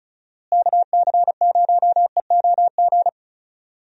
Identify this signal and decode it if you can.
Morse code, heard as a keyed tone, about 34 wpm, 700 Hz. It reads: KC0EOG
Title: USA Call Signs - 6 Characters - Repeated 3x - 35wpm